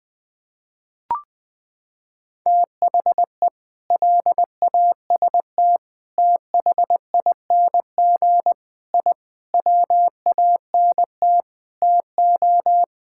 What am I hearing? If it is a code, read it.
Morse code, heard as a keyed tone, about 20 wpm, 700 Hz. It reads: THE LAST THING I WANT TO
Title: Sentences with 7-Word Phrases - 20wpm